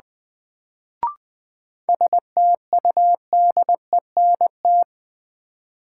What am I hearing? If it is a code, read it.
Morse code, heard as a keyed tone, about 21 wpm, 700 Hz. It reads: STUDENT
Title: Sentences from Top 1000 Words - New Word Drill - 20wpm